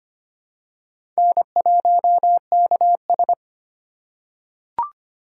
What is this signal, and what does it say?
Morse code, heard as a keyed tone, about 25 wpm, 700 Hz. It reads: N1KS
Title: USA Call Signs - 4 Characters - 4x Character Spacing - 25wpm